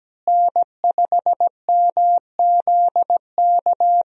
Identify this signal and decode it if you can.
Morse code, heard as a keyed tone, about 17 wpm, 700 Hz. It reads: N5MZK
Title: USA Call Signs - 5 Characters - Repeated 2x - 17wpm